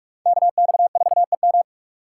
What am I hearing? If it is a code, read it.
Morse code, heard as a keyed tone, about 57 wpm, 700 Hz. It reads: KX4EM